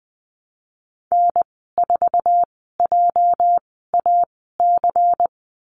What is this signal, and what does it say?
Morse code, heard as a keyed tone, about 20 wpm, spaced as for 14 wpm, 700 Hz. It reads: N4JAC
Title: 231 Lesson 37g - Letter x - Call Signs 20wpm